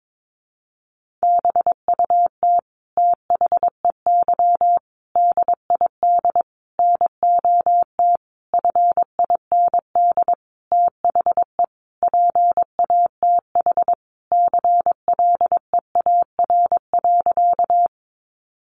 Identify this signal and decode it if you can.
Morse code, heard as a keyed tone, about 22 wpm, 700 Hz. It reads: BUT THEY DID NOT FIND THE PATH CLEAR.
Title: Sentences from Top 1400 Words 22wpm